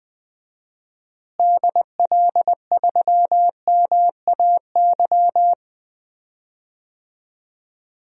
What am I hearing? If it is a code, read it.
Morse code, heard as a keyed tone, about 20 wpm, 700 Hz. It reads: DL3MAY
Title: DXCC-100 Call Signs - 6 Characters - Repeated 3x - 20wpm